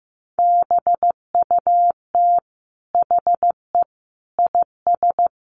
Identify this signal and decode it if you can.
Morse code, heard as a keyed tone, about 14 wpm, 700 Hz. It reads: BUT HE IS
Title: Sentences from Top 200 Words 15wpm